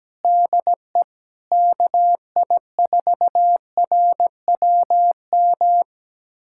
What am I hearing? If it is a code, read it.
Morse code, heard as a keyed tone, about 17 wpm, 700 Hz. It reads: DE KI4RWM
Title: USA Callsign DE Callsign - 6 Characters - 17wpm